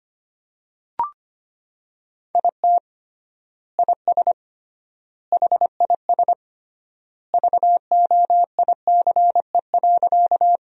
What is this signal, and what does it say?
Morse code, heard as a keyed tone, about 25 wpm, 700 Hz. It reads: IT IS HIS VOICE.
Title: 3x Word-Spacing - Sentences from Top 500 Words - 25wpm